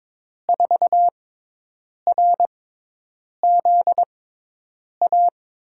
Morse code, heard as a keyed tone, about 21 wpm, spaced as for 8 wpm, 700 Hz. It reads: 4RZA
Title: USA Call Signs - 5 Characters - 6x Character Spacing - 22wpm